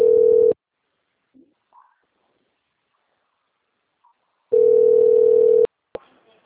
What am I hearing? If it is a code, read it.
Ringback tone (North America), 440 + 480 Hz.
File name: out- - - - - .25254